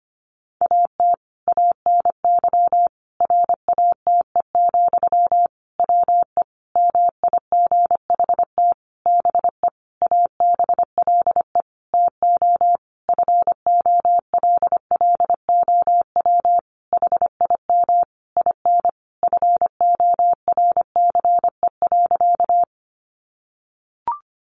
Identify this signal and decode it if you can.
Morse code, heard as a keyed tone, about 25 wpm, 700 Hz. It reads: AT ANY RATE, WE MIGHT BE ABLE TO FOLLOW HIM IN FORCE.
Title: Sentences from Top 700 Words 25wpm